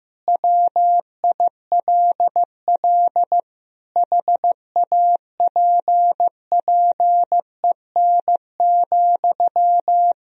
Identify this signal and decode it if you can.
Morse code, heard as a keyed tone, about 15 wpm, 700 Hz. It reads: WILL HAPPEN,
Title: Sentences from Top 1000 Words - Review Part 4 - 15wpm